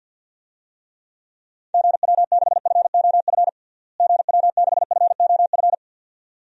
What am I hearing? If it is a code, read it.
Morse code, heard as a keyed tone, about 49 wpm, 700 Hz. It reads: GW6FOF GW6FOF
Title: DXCC-100 Call Signs - 6 Characters - Repeated 2x - 50wpm